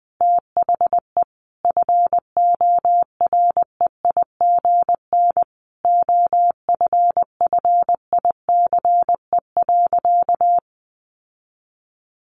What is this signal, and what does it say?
Morse code, heard as a keyed tone, about 20 wpm, 700 Hz. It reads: THE FOREIGN OFFICE.
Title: Sentences from Top 1400 Words 20wpm